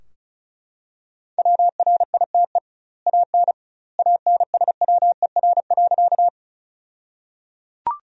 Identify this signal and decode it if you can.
Morse code, heard as a keyed tone, about 36 wpm, 700 Hz. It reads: WRITE AN ANSWER.
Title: 2x Word-Spacing - Sentences from Top 300 Words - 35wpm